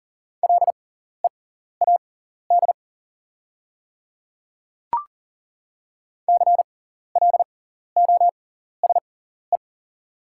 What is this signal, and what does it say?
Morse code, heard as a keyed tone, about 40 wpm, spaced as for 14 wpm, 690 Hz. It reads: LEAD CLOSE
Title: 6x Character Spacing - Top 500 Words - 40wpm